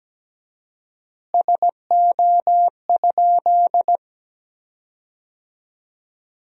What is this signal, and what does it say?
Morse code, heard as a keyed tone, about 17 wpm, 700 Hz. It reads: SO?
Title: 4x Word-Spacing - Sentences from Top 400 Words - 17wpm